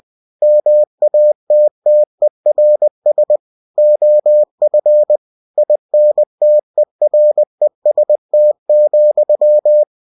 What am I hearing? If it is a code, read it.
Morse code, heard as a keyed tone, about 20 wpm, 600 Hz. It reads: MATTERS OF INTEREST,
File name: cw-books / winnie-the-pooh / winnie-20wpm-page-0022